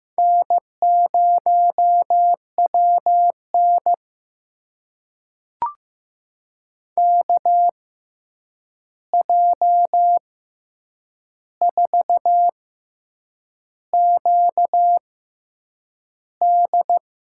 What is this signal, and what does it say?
Morse code, heard as a keyed tone, about 15 wpm, spaced as for 7 wpm, 700 Hz. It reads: N0WN KJ4QD
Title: USA Call Signs - 6x Character Spacing - 15wpm